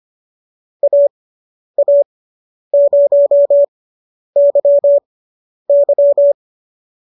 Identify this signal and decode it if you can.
Morse code, heard as a keyed tone, about 25 wpm, spaced as for 10 wpm, 590 Hz. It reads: AA0YY